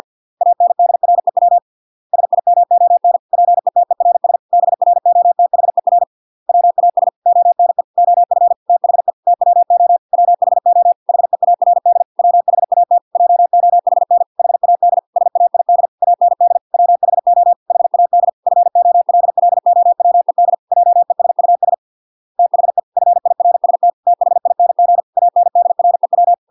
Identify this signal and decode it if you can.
Morse code, heard as a keyed tone, about 50 wpm, 700 Hz. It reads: ANDREW SIMON PETERS BROTHER WAS ONE OF THE TWO WHO HEARD WHAT JOHN HAD SAID AND WHO HAD FOLLOWED JESUS THE FIRST THING ANDREW